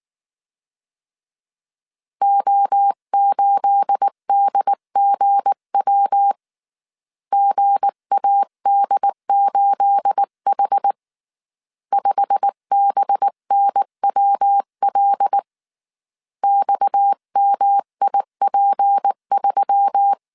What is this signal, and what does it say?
Morse code, heard as a keyed tone, about 19 wpm, 800 Hz. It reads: O8DGW GAD8H 5BNWL XMIP3
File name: CW / DET / 1 9 1500C S C 18wpm 2539